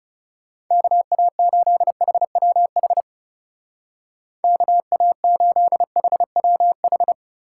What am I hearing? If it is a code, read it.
Morse code, heard as a keyed tone, about 37 wpm, 700 Hz. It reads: KA8HWH KA8HWH
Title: Speed Racing - USA 2x3 Call Signs 30wpm